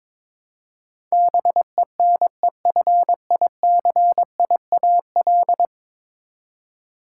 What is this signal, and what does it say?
Morse code, heard as a keyed tone, about 21 wpm, 700 Hz. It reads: BENEFICIAL
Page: morse-code-ninja-practice-22wpm